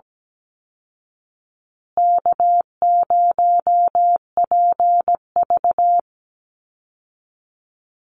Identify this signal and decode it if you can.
Morse code, heard as a keyed tone, about 17 wpm, 700 Hz. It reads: K0PV
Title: Easy Call Signs Repeated 2x - 17wpm